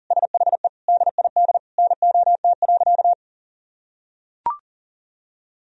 Morse code, heard as a keyed tone, about 40 wpm, 700 Hz. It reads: SHE DID NOT.
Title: Sentences from Top 1000 Words - Review Part 1 - 40wpm